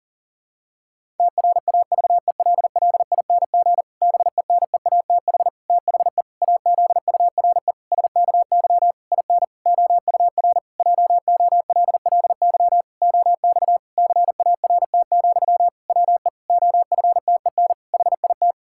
Morse code, heard as a keyed tone, about 40 wpm, 700 Hz. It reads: TRAVELLING BENEATH THE AZURE SKY IN OUR JOLLY OX CART, WE OFTEN HIT
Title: Pangram Challenge v2 - 40wpm